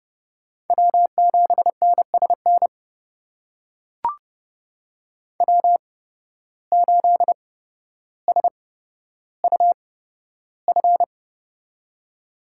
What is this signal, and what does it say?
Morse code, heard as a keyed tone, about 31 wpm, spaced as for 13 wpm, 700 Hz. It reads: W7NSN W8SUF
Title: USA Call Signs - 5 Characters - 8x Character Spacing - 30wpm